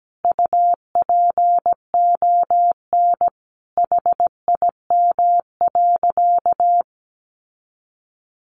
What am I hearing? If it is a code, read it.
Morse code, heard as a keyed tone, about 17 wpm, 700 Hz. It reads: UPON HIM.
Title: Sentences from Top 1100 Words 17wpm